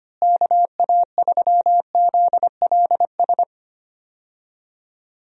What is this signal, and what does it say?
Morse code, heard as a keyed tone, about 25 wpm, 700 Hz. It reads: KA3ZLS
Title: USA Call Signs - 6 Characters - Repeated 3x - 25wpm